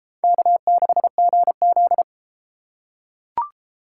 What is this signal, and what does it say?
Morse code, heard as a keyed tone, about 30 wpm, 700 Hz. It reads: K6GZ